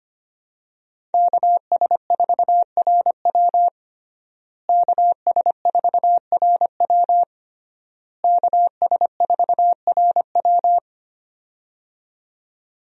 Morse code, heard as a keyed tone, about 25 wpm, 700 Hz. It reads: KS4RW KS4RW KS4RW
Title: USA Call Signs - 5 Characters - Repeated 3x - 25wpm